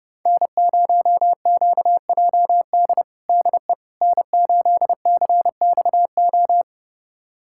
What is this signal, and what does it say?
Morse code, heard as a keyed tone, about 30 wpm, 700 Hz. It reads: N0QJD DE N8CXO
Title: USA Callsign DE Callsign - 5 Characters - 30wpm